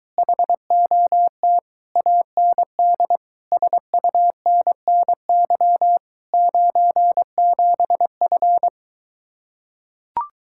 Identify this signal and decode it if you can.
Morse code, heard as a keyed tone, about 23 wpm, 700 Hz. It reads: HOT AND SUNNY 97F